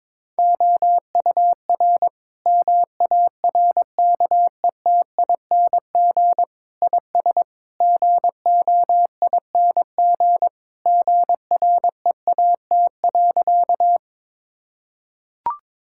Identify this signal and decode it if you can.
Morse code, heard as a keyed tone, about 22 wpm, 700 Hz. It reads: OUR MARKETING IS GOING GREAT.
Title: Sentences from Top 1000 Words - Review Part 5 - 22wpm